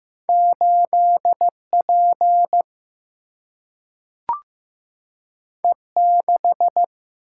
Morse code, heard as a keyed tone, about 15 wpm, 700 Hz. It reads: 8 P E 6
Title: Single-Letter Single-Number 15wpm